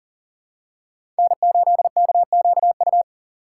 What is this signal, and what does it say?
Morse code, heard as a keyed tone, about 40 wpm, 700 Hz. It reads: N8KQU
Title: USA Call Signs - 5 Characters - 3x Character Spacing - 40wpm